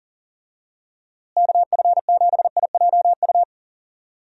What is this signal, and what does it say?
Morse code, heard as a keyed tone, about 40 wpm, 700 Hz. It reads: KF7IJU
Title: USA Call Signs - 6 Characters - Repeated 3x - 40wpm